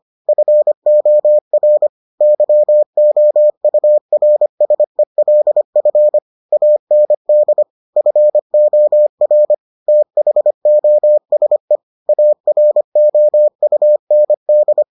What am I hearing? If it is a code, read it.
Morse code, heard as a keyed tone, about 25 wpm, 600 Hz. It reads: FOR YOURSELF AND FOR THOSE AROUND